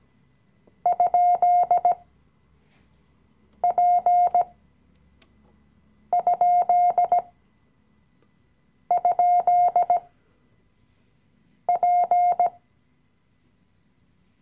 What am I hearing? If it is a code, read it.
Morse code, heard as a keyed tone, about 17 wpm, spaced as for 5 wpm, 710 Hz. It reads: ?P??P
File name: CW-Online-Kurs / Anfänger / Lektion 018